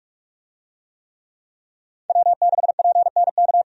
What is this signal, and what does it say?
Morse code, heard as a keyed tone, about 46 wpm, 700 Hz. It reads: W6PNK
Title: USA Call Signs - 5 Characters - 45wpm